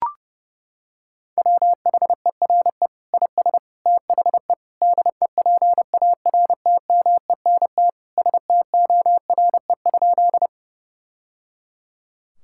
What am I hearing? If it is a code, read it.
Morse code, heard as a keyed tone, about 32 wpm, 700 Hz. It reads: WHERE IS THE DEPARTMENT STORE?
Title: Sentences from Top 1000 Words - Review Part 5 - 30wpm